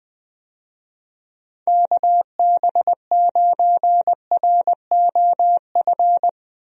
Morse code, heard as a keyed tone, about 20 wpm, 700 Hz. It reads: KB9ROF
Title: USA Call Signs - 6 Characters - Repeated 3x - 20wpm